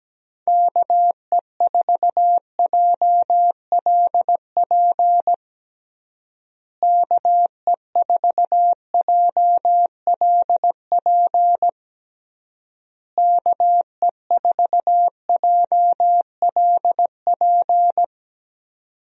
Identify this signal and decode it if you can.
Morse code, heard as a keyed tone, about 17 wpm, 700 Hz. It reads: KE4JLP KE4JLP KE4JLP
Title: USA Call Signs - 6 Characters - Repeated 3x - 17wpm